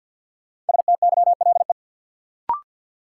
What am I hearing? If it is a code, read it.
Morse code, heard as a keyed tone, about 47 wpm, 700 Hz. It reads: STYLE
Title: Sentences from Top 800 Words - New Word Drill - 50wpm